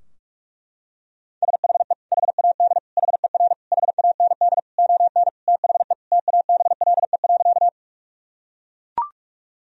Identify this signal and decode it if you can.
Morse code, heard as a keyed tone, about 41 wpm, 700 Hz. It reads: SHE HAD HER HAND ON THE TABLE.
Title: Sentences from Top 1000 Words - Review Part 2 - 45wpm